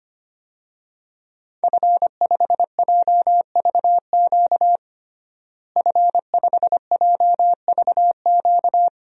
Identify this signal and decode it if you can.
Morse code, heard as a keyed tone, about 25 wpm, 700 Hz. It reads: F5JVQ F5JVQ
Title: DXCC-100 Call Signs - 5 Characters - Repeated 2x - 25wpm